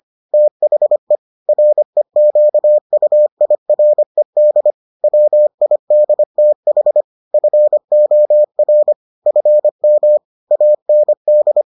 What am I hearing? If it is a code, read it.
Morse code, heard as a keyed tone, about 25 wpm, 600 Hz. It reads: THE REQUIRED WIDTH FOR FM AND